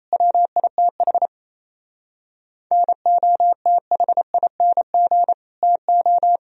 Morse code, heard as a keyed tone, about 34 wpm, 700 Hz. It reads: WITH NOTHING TO
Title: Speed-Racing - Sets of 4 Words 28wpm